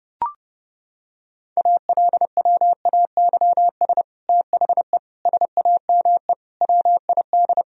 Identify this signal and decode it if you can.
Morse code, heard as a keyed tone, about 32 wpm, 700 Hz. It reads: ALWAYS THE SAME WID
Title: Sentences from Top 300 Words and Abbreviations 30wpm